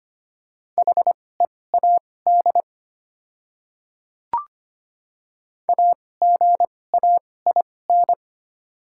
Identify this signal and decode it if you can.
Morse code, heard as a keyed tone, about 25 wpm, spaced as for 18 wpm, 700 Hz. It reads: HEAD AGAIN
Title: 2x Character Spacing - Top 300 Words - 25wpm